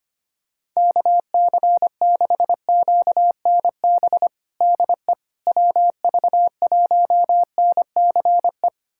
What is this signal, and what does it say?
Morse code, heard as a keyed tone, about 25 wpm, 700 Hz. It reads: KC6QNB DE WV1NCE
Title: USA Callsign DE Callsign - 6 Characters - 25wpm